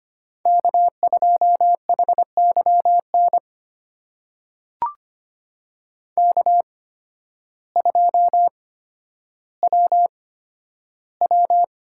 Morse code, heard as a keyed tone, about 25 wpm, spaced as for 12 wpm, 700 Hz. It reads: K2HYN K2WW
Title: USA Call Signs - 8x Character Spacing - 25wpm